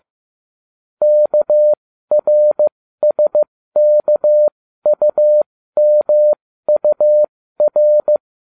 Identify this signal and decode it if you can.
Morse code, heard as a keyed tone, about 15 wpm, spaced as for 13 wpm, 600 Hz. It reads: KRSKUMUR